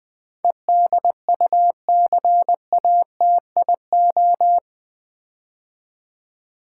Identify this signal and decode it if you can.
Morse code, heard as a keyed tone, about 20 wpm, 700 Hz. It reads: EDUCATIO
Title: Build A Word - 4 Letters At A Time - 11-Letter Words - 20wpm